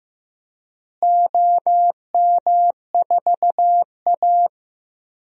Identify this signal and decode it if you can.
Morse code, heard as a keyed tone, about 15 wpm, 700 Hz. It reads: OM4A
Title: DXCC-100 Call Signs - 4 Characters - Repeated 2x - 15wpm